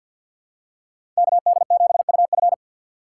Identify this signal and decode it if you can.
Morse code, heard as a keyed tone, about 48 wpm, 700 Hz. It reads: KD7UF
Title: USA Call Signs - 5 Characters - 3x Character Spacing - 50wpm